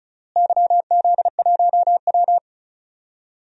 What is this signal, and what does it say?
Morse code, heard as a keyed tone, about 34 wpm, 700 Hz. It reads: YZ1W
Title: DXCC-100 Call Signs - 4 Characters - Repeated 3x - 35wpm